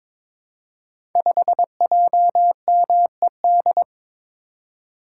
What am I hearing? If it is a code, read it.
Morse code, heard as a keyed tone, about 22 wpm, 700 Hz. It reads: 5JMED
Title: Code Groups - 5 Letter:Numbers - 22wpm